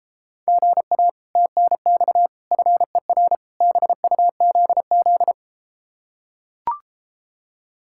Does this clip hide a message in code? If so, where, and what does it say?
Yes: Morse code from 0.5 to 5.3 s, heard as a keyed tone, about 33 wpm, 700 Hz. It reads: GA TNX FER BUZZ